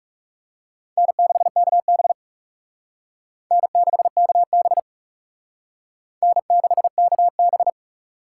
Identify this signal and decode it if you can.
Morse code, heard as a keyed tone, about 38 wpm, 700 Hz. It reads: N6KB N6KB N6KB